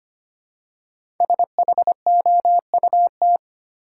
Morse code, heard as a keyed tone, about 25 wpm, 700 Hz. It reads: SHOUT
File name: 5 Letter Words - First 3 Letters Given - 25wpm